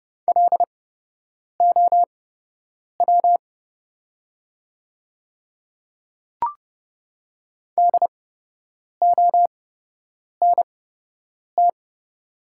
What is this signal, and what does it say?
Morse code, heard as a keyed tone, about 30 wpm, spaced as for 8 wpm, 700 Hz. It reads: LOW DONT
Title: 8x Character Spacing - Top 300 Words - 30wpm